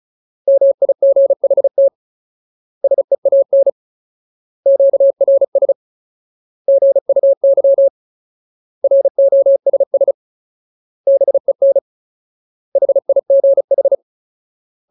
Morse code, heard as a keyed tone, about 34 wpm, 550 Hz. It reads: MIGHT SEAN QRS GUY ROSS BEN HIGH
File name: Ultimate Word List / 35WPM Ultimate